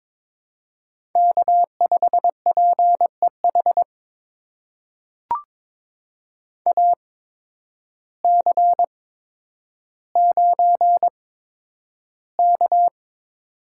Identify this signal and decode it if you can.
Morse code, heard as a keyed tone, about 22 wpm, spaced as for 10 wpm, 700 Hz. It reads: K5PEH AC9K
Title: USA Call Signs - 8x Character Spacing - 22wpm